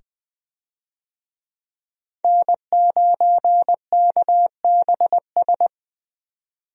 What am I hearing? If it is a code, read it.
Morse code, heard as a keyed tone, about 20 wpm, 700 Hz. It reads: N9KBS
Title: USA Call Signs - 5 Characters - 20wpm